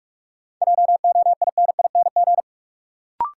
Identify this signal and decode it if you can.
Morse code, heard as a keyed tone, about 44 wpm, 700 Hz. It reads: JOINING
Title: Sets of 5 Words - New Word Drill - 45wpm